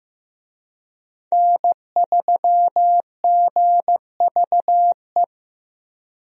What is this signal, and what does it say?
Morse code, heard as a keyed tone, about 15 wpm, 700 Hz. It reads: N3GVE
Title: USA Call Signs - 5 Characters - Repeated 3x - 15wpm